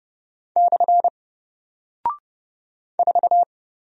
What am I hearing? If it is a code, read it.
Morse code, heard as a keyed tone, about 30 wpm, 700 Hz. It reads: / 4
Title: Single Character (Letter:Number:Punctuation:Prosign) Alt 30wpm